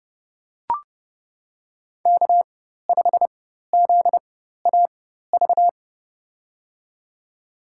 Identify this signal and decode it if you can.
Morse code, heard as a keyed tone, about 32 wpm, spaced as for 14 wpm, 700 Hz. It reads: K5ZAV
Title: USA Call Signs - 4x Character Spacing - 30wpm